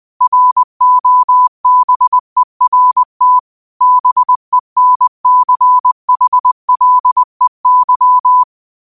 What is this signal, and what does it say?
Morse code, heard as a keyed tone, about 20 wpm, 1000 Hz. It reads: ROBERT BENCHLEY
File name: qotdfiles / 20wpm